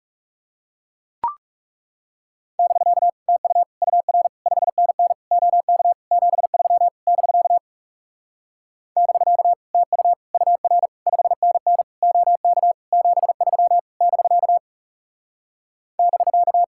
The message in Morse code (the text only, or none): <BK> TU UR 5NN OK 73 <BK> <BK> TU UR 5NN OK 73 <BK> <BK>